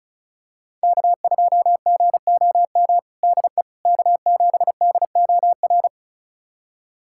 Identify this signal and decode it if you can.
Morse code, heard as a keyed tone, about 34 wpm, 700 Hz. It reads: K2GOM DE K7DOR
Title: USA Callsign DE Callsign - 5 Characters - 35wpm